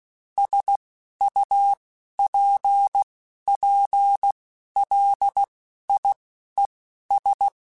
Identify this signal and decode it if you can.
Morse code, heard as a keyed tone, about 16 wpm, spaced as for 11 wpm, 800 Hz. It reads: SUPPLIES
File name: Technical Subjects / Technical Subjects-mp3s / Sw Pwr Sup @10